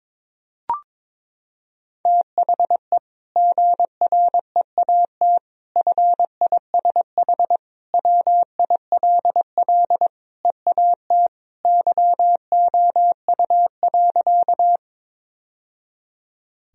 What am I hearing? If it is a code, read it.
Morse code, heard as a keyed tone, about 22 wpm, 700 Hz. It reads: THE GREAT FISH WILL EAT YOU.
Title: Sentences from Top 1000 Words - Review Part 1 - 22wpm